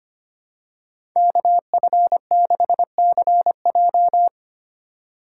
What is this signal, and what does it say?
Morse code, heard as a keyed tone, about 25 wpm, 700 Hz. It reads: KF6CJ
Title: USA Call Signs - 3x Character Spacing - 25wpm